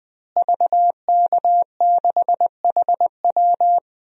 Morse code, heard as a keyed tone, about 20 wpm, 700 Hz. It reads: VK6HW
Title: DXCC-100 Call Signs - 5 Characters - Repeated 2x - 20wpm